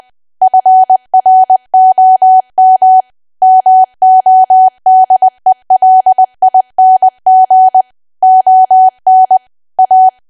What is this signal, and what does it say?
Morse code, heard as a keyed tone, about 20 wpm, 750 Hz. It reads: FROM MODELING ON A
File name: cwa / W1AW-Level3 / 20WPM